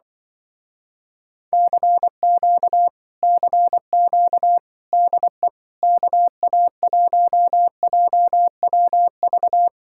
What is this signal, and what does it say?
Morse code, heard as a keyed tone, about 24 wpm, 700 Hz. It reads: CQ CQ DE KA1JWV
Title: Speed-Racing Common QSO Phrases - 20wpm